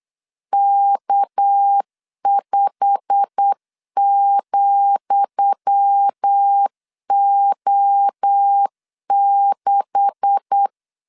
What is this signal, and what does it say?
Morse code, heard as a keyed tone, about 8 wpm, 800 Hz. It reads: K5,O6
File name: LEZIONI / 10 SIGNS / 10D COMA 8wpm 3749